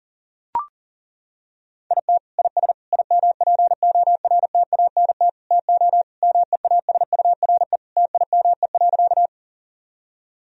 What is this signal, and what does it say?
Morse code, heard as a keyed tone, about 40 wpm, 700 Hz. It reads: IT IS IMPORTANT TO MEASURE TIME.
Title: Sentences from Top 1000 Words - Review Part 2 - 40wpm